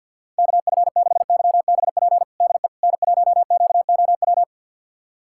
Morse code, heard as a keyed tone, about 52 wpm, 700 Hz. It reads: KF6YBP DE N1QOW
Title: USA Callsign DE Callsign 50wpm